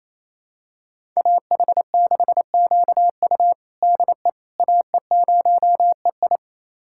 Morse code, heard as a keyed tone, about 28 wpm, 700 Hz. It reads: AH6QU DE AE0EI
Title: USA Callsign DE Callsign - 5 Characters - 28wpm